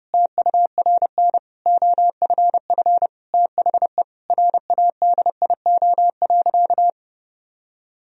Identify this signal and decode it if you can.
Morse code, heard as a keyed tone, about 30 wpm, 700 Hz. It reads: TURN OFF THE RADIO.
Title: Sentences from Top 1000 Words - Review Part 3 - 30wpm